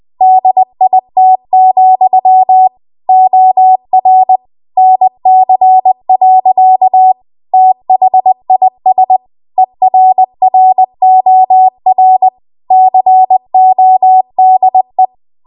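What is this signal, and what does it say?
Morse code, heard as a keyed tone, about 20 wpm, 750 Hz. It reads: DIT, OR NC. THIS ERROR CODE
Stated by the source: Morse code practice at 20 WPM.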